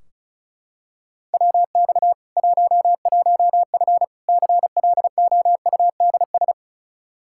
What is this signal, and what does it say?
Morse code, heard as a keyed tone, about 36 wpm, 700 Hz. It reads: WX 11F CLOUDS
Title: Practice QSO - Exchange 2 - Info - 35wpm